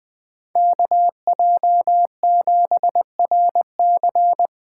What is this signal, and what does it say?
Morse code, heard as a keyed tone, about 20 wpm, 700 Hz. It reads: KJ7RC
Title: Speed Racing - USA Call Signs 17wpm